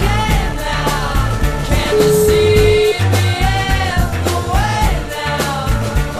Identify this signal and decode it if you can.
Ringback tone (Europe, 425 Hz), with noise.